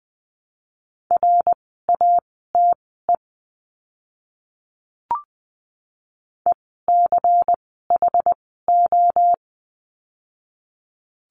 Morse code, heard as a keyed tone, about 20 wpm, spaced as for 14 wpm, 700 Hz. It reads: RATE ECHO
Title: 094 Lesson 16f - Letter c - Words 20wpm